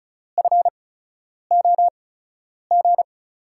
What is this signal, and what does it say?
Morse code, heard as a keyed tone, about 36 wpm, spaced as for 10 wpm, 700 Hz. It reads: FOG
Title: USA Call Signs - 5 Characters - 8x Character Spacing - 35wpm